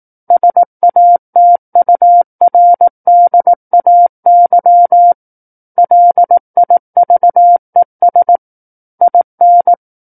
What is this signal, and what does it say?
Morse code, heard as a keyed tone, about 18 wpm, 700 Hz. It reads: SATURDAY LIVES IN
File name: 18 WPM